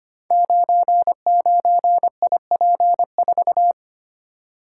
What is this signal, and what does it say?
Morse code, heard as a keyed tone, about 25 wpm, 700 Hz. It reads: 99IP4
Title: Code Groups - 5 Letter:Numbers - 25wpm